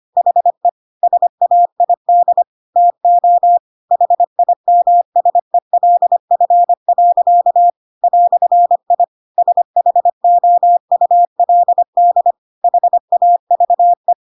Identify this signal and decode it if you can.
Morse code, heard as a keyed tone, about 25 wpm, 700 Hz. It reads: HE SAID TO HIMSELF. "I SHOULD HAVE
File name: Pinocchio / Pinocchio ch3 25 25 ik1hge